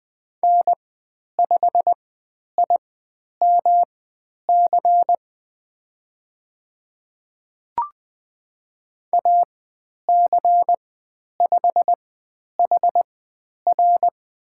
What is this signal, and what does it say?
Morse code, heard as a keyed tone, about 20 wpm, spaced as for 10 wpm, 700 Hz. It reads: N5IMC AC5HR
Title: 109 Lesson 18g - Letter m - Call Signs 20wpm